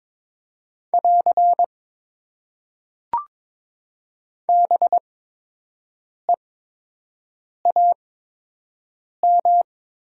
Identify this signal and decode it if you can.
Morse code, heard as a keyed tone, about 22 wpm, spaced as for 6 wpm, 700 Hz. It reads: <AR> BEAM
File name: Top 90 QSO Elements - 8x Character Spacing - 22wpm